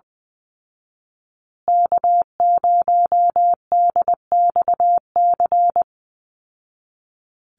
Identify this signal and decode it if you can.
Morse code, heard as a keyed tone, about 20 wpm, 700 Hz. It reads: K0DXC
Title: Call Signs Repeated 2x - 20wpm